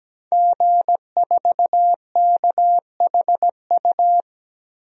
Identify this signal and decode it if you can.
Morse code, heard as a keyed tone, about 17 wpm, 700 Hz. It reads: G4KHU
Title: DXCC-100 Call Signs - 5 Characters - Repeated 3x - 17wpm